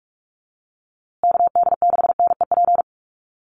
Morse code, heard as a keyed tone, about 44 wpm, 690 Hz. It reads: KD6NEP
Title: USA Call Signs Repeated 3x 45wpm